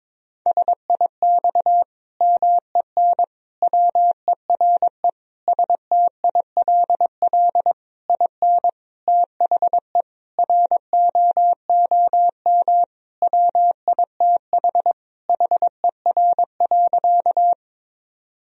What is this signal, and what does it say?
Morse code, heard as a keyed tone, about 22 wpm, 700 Hz. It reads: SIX MEN WERE STILL IN THE ROOM WITH HER.
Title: Sentences from Top 1000 Words - Review Part 2 - 22wpm